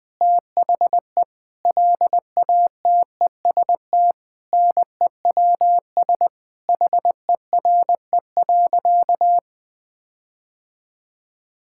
Morse code, heard as a keyed tone, about 21 wpm, 700 Hz. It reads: THE LATEST NEWS HERE.
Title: Sentences with 9-Word Phrases - Spoken Beginning - 20wpm